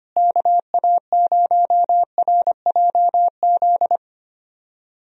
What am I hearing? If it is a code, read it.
Morse code, heard as a keyed tone, about 25 wpm, 700 Hz. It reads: KA0RJZ
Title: USA Call Signs - 6 Characters - Repeated 3x - 25wpm